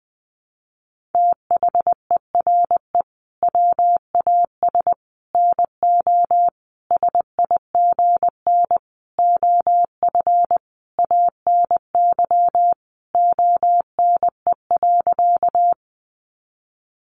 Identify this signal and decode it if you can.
Morse code, heard as a keyed tone, about 20 wpm, 700 Hz. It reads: THERE WAS NO SIGN OF ANY ONE.
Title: Sentences from Top 1000 Words 20wpm